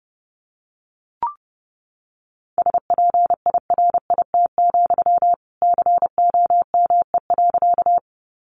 Morse code, heard as a keyed tone, about 30 wpm, 700 Hz. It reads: SPIRIT, COME.
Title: Sentences from Top 1400 Words 30wpm